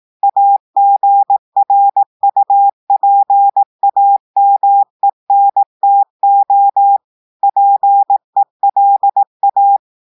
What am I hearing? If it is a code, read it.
Morse code, heard as a keyed tone, about 18 wpm, 800 Hz. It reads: AGRUPAMENTO PELA
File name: audios / portugais / portugais texte en clair 1 vitesse 18 mots minutes